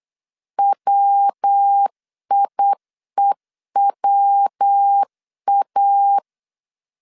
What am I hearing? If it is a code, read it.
Morse code, heard as a keyed tone, about 8 wpm, 800 Hz. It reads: WIEWA
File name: CW / DET / 1 3 1500C S B 8wpm 4141